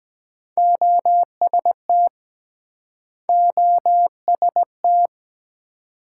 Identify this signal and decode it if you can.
Morse code, heard as a keyed tone, about 24 wpm, 700 Hz. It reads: OST OST